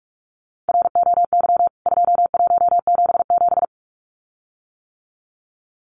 Morse code, heard as a keyed tone, about 44 wpm, 700 Hz. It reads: ROY 2177
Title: CWT Exchange 45wpm